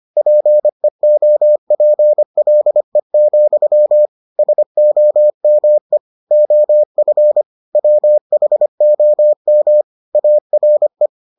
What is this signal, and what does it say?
Morse code, heard as a keyed tone, about 25 wpm, 600 Hz. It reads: PEOPLE, SOME OF WHOM ARE